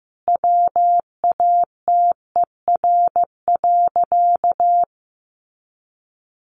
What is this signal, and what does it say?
Morse code, heard as a keyed tone, about 15 wpm, 700 Hz. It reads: WATER.
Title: A Sound Of Thunder 15wpm